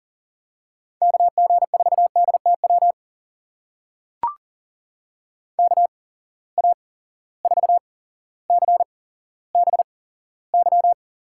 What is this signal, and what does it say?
Morse code, heard as a keyed tone, about 40 wpm, spaced as for 17 wpm, 700 Hz. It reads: KG4DTW KA4CBY
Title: USA Call Signs - 6 Characters - 8x Character Spacing - 40wpm